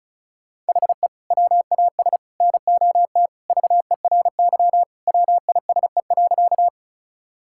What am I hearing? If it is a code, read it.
Morse code, heard as a keyed tone, about 36 wpm, 700 Hz. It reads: HE WAS NOT VERY WISE.
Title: Sentences from Top 1600 Words 35wpm